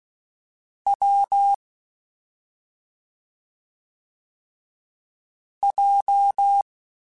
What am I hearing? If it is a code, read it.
Morse code, heard as a keyed tone, about 16 wpm, 800 Hz. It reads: W J
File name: Difficult Combos / Difficult Combos-mp3s / AWJ ShortRun @5